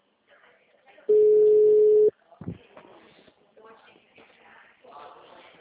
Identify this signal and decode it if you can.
Ringback tone (Europe, 425 Hz), with speech.
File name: out- -2003- - - .23261